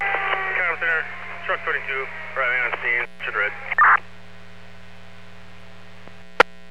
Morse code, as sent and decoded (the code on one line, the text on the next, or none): none